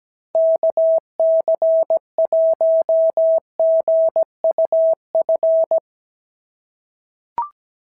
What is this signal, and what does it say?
Morse code, heard as a keyed tone, about 16 wpm, 650 Hz. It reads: KC1GUF
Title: SOTA QSO Practice 17wpm